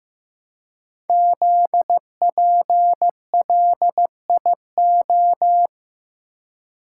Morse code, heard as a keyed tone, about 15 wpm, 700 Hz. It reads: ZPLIO
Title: Code Groups - 5 Letters - 15wpm